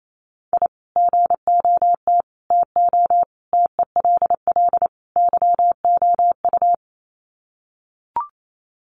Morse code, heard as a keyed tone, about 28 wpm, 700 Hz. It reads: I GOT TO TELL YOU
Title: Sets of 5 Words Encore2 28wpm